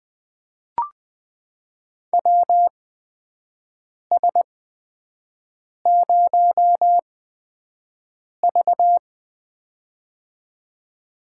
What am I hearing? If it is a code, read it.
Morse code, heard as a keyed tone, about 20 wpm, spaced as for 5 wpm, 700 Hz. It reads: WS0V
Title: USA Call Signs - 4 Characters - 8x Character Spacing - 20wpm